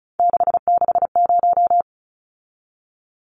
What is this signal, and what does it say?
Morse code, heard as a keyed tone, about 34 wpm, 700 Hz. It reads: 660
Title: Numbers - 3-Digit 35wpm